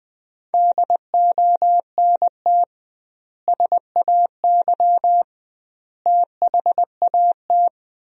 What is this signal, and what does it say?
Morse code, heard as a keyed tone, about 20 wpm, 700 Hz. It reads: DONT SAY THAT
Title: 2x Word-Spacing - Sentences from Top 300 Words - 20wpm